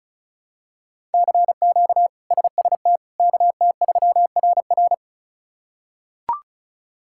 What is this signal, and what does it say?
Morse code, heard as a keyed tone, about 36 wpm, 700 Hz. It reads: CQ SST KT3RR
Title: K1USN SST QSO Practice - 35wpm